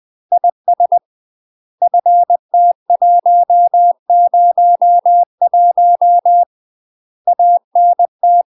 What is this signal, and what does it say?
Morse code, heard as a keyed tone, about 20 wpm, 700 Hz. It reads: IS FT101 ANT
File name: UK MASTER QSOs TRAINING / 20WPM / UK MASTER QSOs 0010